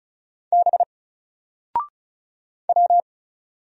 Morse code, heard as a keyed tone, about 36 wpm, 700 Hz. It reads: B W
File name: Single-Letter-Number-35wpm